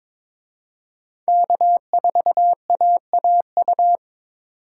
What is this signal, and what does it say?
Morse code, heard as a keyed tone, about 26 wpm, 700 Hz. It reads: K4AAU
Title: Speed Racing - USA 1x3 Call Signs 22wpm